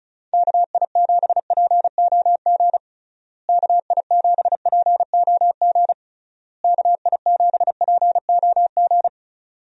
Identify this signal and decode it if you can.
Morse code, heard as a keyed tone, about 34 wpm, 700 Hz. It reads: KI7POG KI7POG KI7POG
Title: USA Call Signs - 6 Characters - Repeated 3x - 35wpm